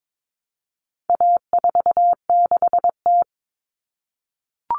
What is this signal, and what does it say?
Morse code, heard as a keyed tone, about 22 wpm, 700 Hz. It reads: A46T
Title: Common Cut Numbers - 4-Digit 22wpm